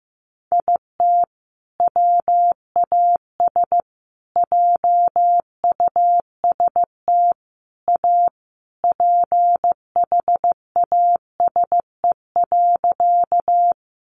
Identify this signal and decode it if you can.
Morse code, heard as a keyed tone, about 15 wpm, 700 Hz. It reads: IT WAS JUST A PHASE.
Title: Sentences from Top 1200 Words 15wpm